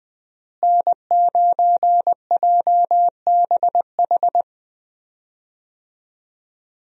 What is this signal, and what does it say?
Morse code, heard as a keyed tone, about 20 wpm, 700 Hz. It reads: N9JBH
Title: USA Call Signs - 5 Characters - Repeated 2x - 20wpm